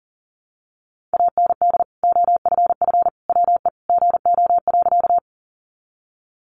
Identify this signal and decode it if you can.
Morse code, heard as a keyed tone, about 40 wpm, 700 Hz. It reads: AND OFF WE GO.
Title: Sentences from Top 200 Words 40wpm